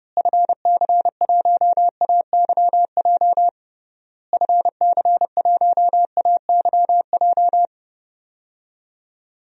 Morse code, heard as a keyed tone, about 32 wpm, 700 Hz. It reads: FC1AYJ FC1AYJ
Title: DXCC-100 Call Signs - 6 Characters - Repeated 3x - 30wpm